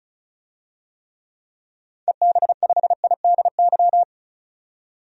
Morse code, heard as a keyed tone, about 35 wpm, 700 Hz. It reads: EB5IDY
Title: DXCC-100 Call Signs - 6 Characters - 35wpm